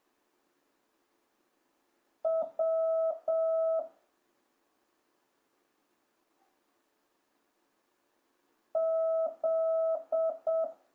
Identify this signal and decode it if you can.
Morse code, heard as a keyed tone, about 7 wpm, 650 Hz. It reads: W Z